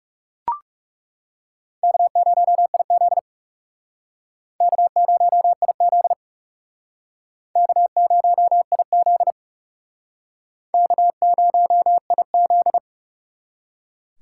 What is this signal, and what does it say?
Morse code, heard as a keyed tone, about 39 wpm, 700 Hz. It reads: K0IZ K0IZ K0IZ K0IZ